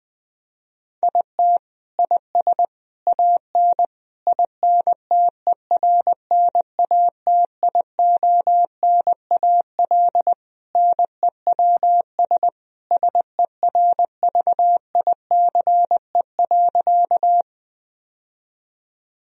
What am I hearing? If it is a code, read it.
Morse code, heard as a keyed tone, about 20 wpm, 700 Hz. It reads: IT IS AN INTERNATIONAL NEWS SERVICE.
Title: Sentences from Top 1000 Words - Review Part 4 - 20wpm